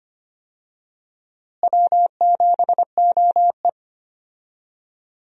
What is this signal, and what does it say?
Morse code, heard as a keyed tone, about 25 wpm, 700 Hz. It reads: W7OE
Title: USA Call Signs - 4 Characters - Repeated 3x - 25wpm